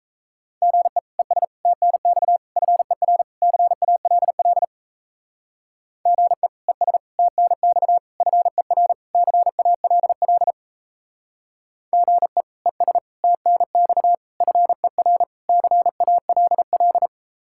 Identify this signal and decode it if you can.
Morse code, heard as a keyed tone, about 34 wpm, 700 Hz. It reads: GE ES TNX FER CALL GE ES TNX FER CALL GE ES TNX FER CALL
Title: Speed-Racing Common QSO Phrases - 28wpm